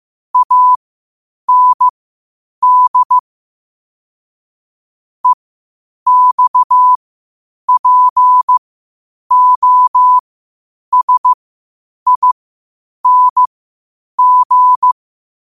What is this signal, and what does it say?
Morse code, heard as a keyed tone, about 15 wpm, spaced as for 8 wpm, 1000 Hz. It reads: AND EXPOSING
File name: qotdfiles / 8wpm